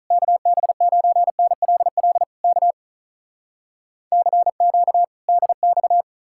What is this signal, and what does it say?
Morse code, heard as a keyed tone, about 46 wpm, 700 Hz. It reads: KB9NLL K CQ DX